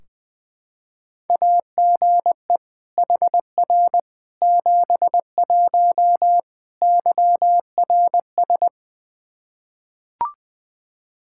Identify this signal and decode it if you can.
Morse code, heard as a keyed tone, about 20 wpm, 700 Hz. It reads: AGE HR 71 YRS
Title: Practice QSO - Exchange 3 - More Info - 20wpm